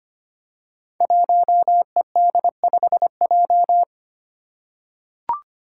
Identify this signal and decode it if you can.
Morse code, heard as a keyed tone, about 25 wpm, 700 Hz. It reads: 1ED5J
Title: Five Challenging Letter-Number Combinations 25wpm